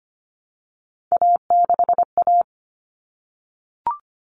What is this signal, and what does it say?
Morse code, heard as a keyed tone, about 25 wpm, 700 Hz. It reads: A6A
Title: Common Cut Numbers - 3-Digit 25wpm